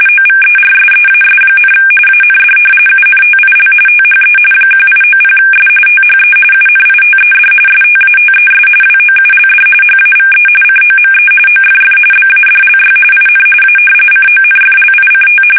RTTY, about 45.45 baud. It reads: OVERSEAS, AND UNTIL THE 1890S THE UNITED STATES WOULD HAVE COME OFF BADLY IN A CONFLICT
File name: fsk850 / usnavy3